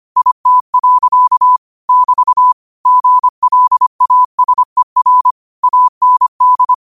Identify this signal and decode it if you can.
Morse code, heard as a keyed tone, about 25 wpm, 1000 Hz. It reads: IT. <BT> GLASER AND
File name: qotdfiles / 25wpm